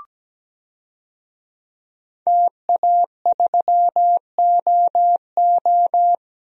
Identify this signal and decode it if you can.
Morse code, heard as a keyed tone, about 17 wpm, 700 Hz. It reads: TA3OO
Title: DXCC-100 Call Signs - 5 Characters - 17wpm